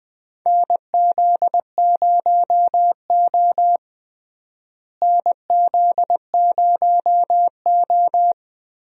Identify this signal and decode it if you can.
Morse code, heard as a keyed tone, about 20 wpm, 700 Hz. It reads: NZ0O NZ0O
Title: USA Call Signs - 4 Characters - Repeated 2x - 20wpm